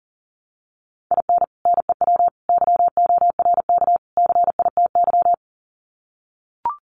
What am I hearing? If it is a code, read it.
Morse code, heard as a keyed tone, about 39 wpm, 700 Hz. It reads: IN NEW YORK CITY
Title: Sets of 4 Words 40wpm